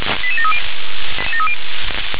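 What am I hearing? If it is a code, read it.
Five-tone selective call (ZVEI), heard as quick runs of five tones, tones of about 70 ms. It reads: 88730
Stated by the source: ZVEI five-tone selective call received on an SDR.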